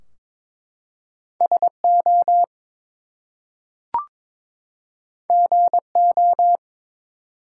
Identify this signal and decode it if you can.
Morse code, heard as a keyed tone, about 22 wpm, 700 Hz. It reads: SO GO
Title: Two Letter Words - 22wpm